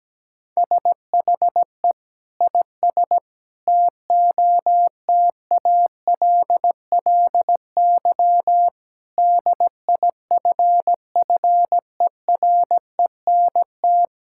Morse code, heard as a keyed tone, about 17 wpm, 700 Hz. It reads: SHE IS TOTALLY DIFFERENT
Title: Sentences from Top 1600 Words 17wpm